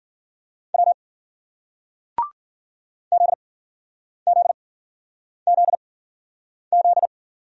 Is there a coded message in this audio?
Yes: Morse code from 0.7 to 7.1 s, heard as a keyed tone, about 54 wpm, spaced as for 9 wpm, 700 Hz. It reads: W ZZZZ